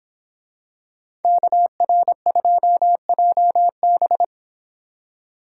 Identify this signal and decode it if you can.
Morse code, heard as a keyed tone, about 24 wpm, 700 Hz. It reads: KR2JB